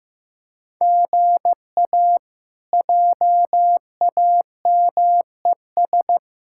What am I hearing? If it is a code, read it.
Morse code, heard as a keyed tone, about 15 wpm, 700 Hz. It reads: GA JAMES
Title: K1USN SST QSO Practice - 15wpm